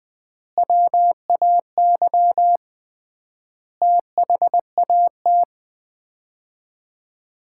Morse code, heard as a keyed tone, about 20 wpm, 700 Hz. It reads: WAY THAT
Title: 3x Word-Spacing - Sets of 5 Words - 20wpm